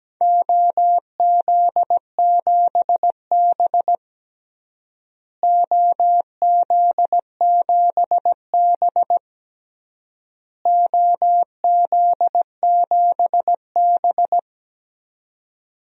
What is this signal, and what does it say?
Morse code, heard as a keyed tone, about 17 wpm, 700 Hz. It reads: OZ7B OZ7B OZ7B
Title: DXCC-100 Call Signs - 4 Characters - Repeated 3x - 17wpm